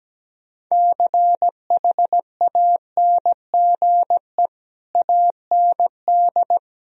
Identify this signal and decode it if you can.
Morse code, heard as a keyed tone, about 17 wpm, 700 Hz. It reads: CHANGE AND
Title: Sentences from Top 1000 Words - Review Part 1 - 17wpm